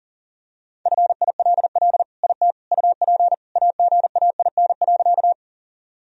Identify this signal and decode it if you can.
Morse code, heard as a keyed tone, about 40 wpm, 700 Hz. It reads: FILL IT UP AGAIN.
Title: Sentences from Top 1000 Words - Review Part 3 - 40wpm